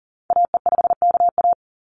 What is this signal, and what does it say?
Morse code, heard as a keyed tone, about 42 wpm, 700 Hz. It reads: AE5KA
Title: NA SOTA Activator Callsigns 40wpm